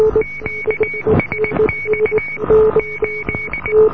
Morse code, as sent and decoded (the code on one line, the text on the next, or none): --.- ... -..- / ---..
QSX 8